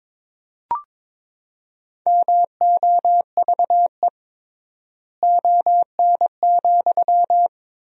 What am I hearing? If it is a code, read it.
Morse code, heard as a keyed tone, about 22 wpm, 700 Hz. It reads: MOVE ON,
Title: 3x Word-Spacing - Sentences from Top 200 Words - 22wpm